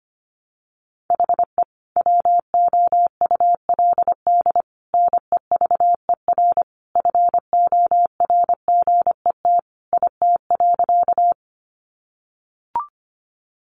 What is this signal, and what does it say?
Morse code, heard as a keyed tone, about 25 wpm, 700 Hz. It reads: HE WOULD NEVER FORGET IT.
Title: Sentences from Top 1200 Words 25wpm